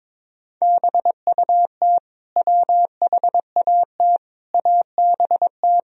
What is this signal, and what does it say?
Morse code, heard as a keyed tone, about 21 wpm, 700 Hz. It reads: BUT WHAT ABT
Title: Sentences from Top 200 Words and Abbreviations 22wpm